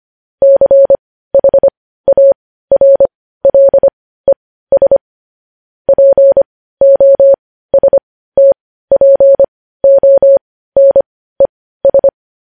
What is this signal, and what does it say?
Morse code, heard as a keyed tone, about 25 wpm, spaced as for 15 wpm, 550 Hz. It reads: CHARLES POSTPONES
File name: podcast / News headlines-1x-25x15